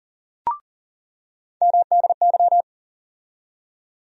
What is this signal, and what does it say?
Morse code, heard as a keyed tone, about 40 wpm, 700 Hz. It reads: MDY
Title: USA Call Sign Postfixes - 3 Letters 40wpm